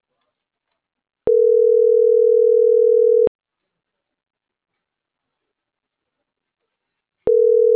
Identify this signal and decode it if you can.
Ringback tone (North America), 440 + 480 Hz.